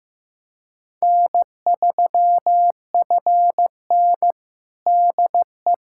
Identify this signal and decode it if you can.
Morse code, heard as a keyed tone, about 15 wpm, 700 Hz. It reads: N3FN DE
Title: USA Callsign DE Callsign - 4 Characters - 15wpm